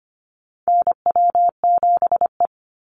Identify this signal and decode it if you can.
Morse code, heard as a keyed tone, about 25 wpm, 700 Hz. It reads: NW7E
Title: NA SOTA Chaser Callsigns 25wpm